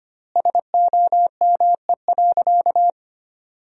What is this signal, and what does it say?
Morse code, heard as a keyed tone, about 25 wpm, 700 Hz. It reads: SOME.
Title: 3x Word-Spacing - Sentences from Top 100 Words - 25wpm